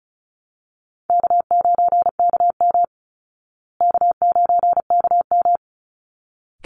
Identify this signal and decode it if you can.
Morse code, heard as a keyed tone, about 34 wpm, 700 Hz. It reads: K9KM K9KM
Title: Easy Call Signs Repeated 2x - 35wpm